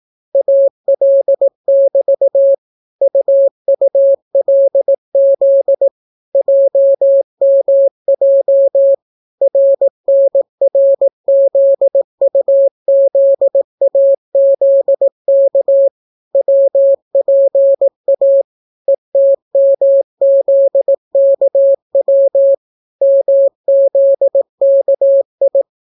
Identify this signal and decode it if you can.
Morse code, heard as a keyed tone, about 18 wpm, 550 Hz. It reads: AL<BT> UULZ JMJ RNRZUZAZK WPA ETMZKW MZKI